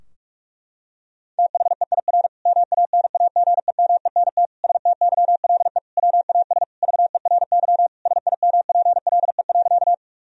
Morse code, heard as a keyed tone, about 44 wpm, 700 Hz. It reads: THEIR MANAGEMENT STYLE WAS VERY SIMPLE.
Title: Sentences from Top 1000 Words - Review Part 4 - 45wpm